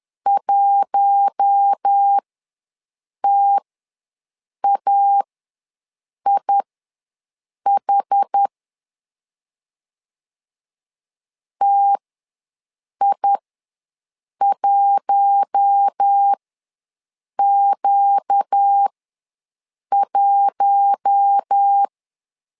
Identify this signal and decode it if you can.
Morse code, heard as a keyed tone, about 10 wpm, spaced as for 6 wpm, 800 Hz. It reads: 1TAIH TI1Q1
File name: LEZIONI / 4 Q1 ws 10wpm 2538